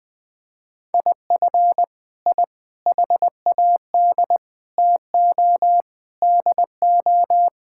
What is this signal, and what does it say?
Morse code, heard as a keyed tone, about 20 wpm, 700 Hz. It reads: IF I HAD TO DO
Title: Sets of 9 Words - 20wpm